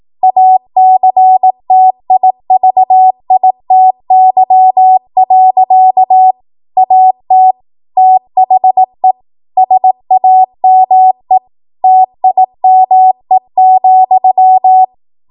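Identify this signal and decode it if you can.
Morse code, heard as a keyed tone, about 18 wpm, 750 Hz. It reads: ACTIVITY. AT THE SAME TIME,
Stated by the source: Code practice at 18 WPM, a transition run bridging 15 to 20 WPM.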